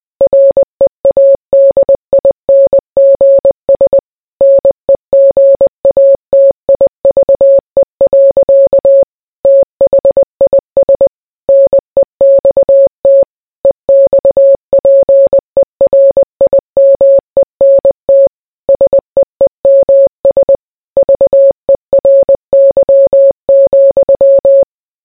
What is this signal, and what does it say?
Morse code, heard as a keyed tone, about 20 wpm, 550 Hz. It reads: READINGS NEGATIVE. THIS NEXT EXPERIMENT SEEMS VERY,